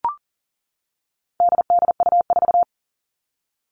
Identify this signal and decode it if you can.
Morse code, heard as a keyed tone, about 37 wpm, 700 Hz. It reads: DDU4
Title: All Cut Numbers - 4-Digit 40wpm